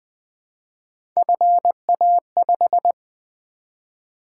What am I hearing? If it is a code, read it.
Morse code, heard as a keyed tone, about 20 wpm, 700 Hz. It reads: FA5
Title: Three Letter-Number Combinations - 20wpm